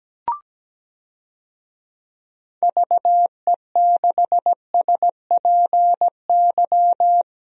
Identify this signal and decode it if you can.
Morse code, heard as a keyed tone, about 17 wpm, 700 Hz. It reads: VE6SPY
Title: DXCC-100 Call Signs - 6 Characters - Repeated 2x - 17wpm